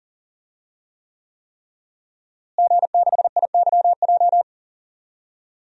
Morse code, heard as a keyed tone, about 40 wpm, 700 Hz. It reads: G6IYJ
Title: DXCC-100 Call Signs - 5 Characters - Repeated 3x - 40wpm